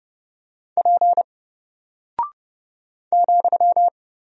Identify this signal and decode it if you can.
Morse code, heard as a keyed tone, about 30 wpm, 700 Hz. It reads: P ,
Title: Single Character (Letter:Number:Punctuation:Prosign) 30wpm